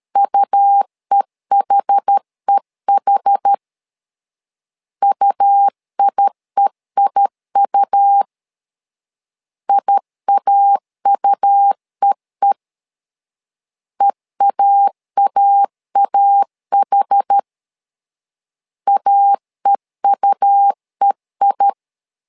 Morse code, heard as a keyed tone, about 13 wpm, 800 Hz. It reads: UEHEH UIEIU IAUEE EAAAH AEUEI
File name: LEZIONI / 2 UHDN / 2 UH 12wpm 1439